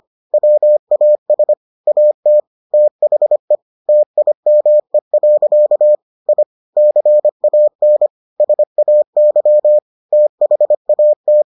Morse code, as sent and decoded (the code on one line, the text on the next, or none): .-- .- ... / .- - / - .... / - .. -- . .-.-.- / .. / -.-. .- -. / ... .- -.-- / - .... .- -
WAS AT THE TIME. I CAN SAY THAT